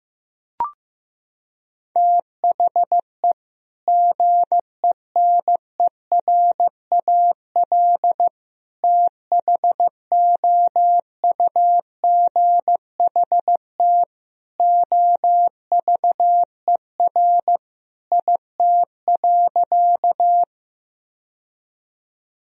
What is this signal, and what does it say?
Morse code, heard as a keyed tone, about 15 wpm, 700 Hz. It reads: THE GENERAL THOUGHT OVER IT.
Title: Sentences from Top 1000 Words - Review Part 3 - 15wpm